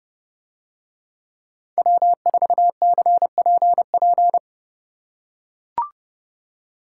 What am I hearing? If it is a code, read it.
Morse code, heard as a keyed tone, about 29 wpm, 700 Hz. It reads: W4CPP